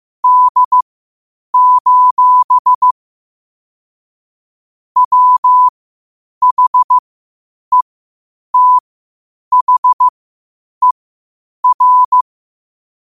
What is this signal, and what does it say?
Morse code, heard as a keyed tone, about 15 wpm, spaced as for 8 wpm, 1000 Hz. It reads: D: WHETHER
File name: 8wpm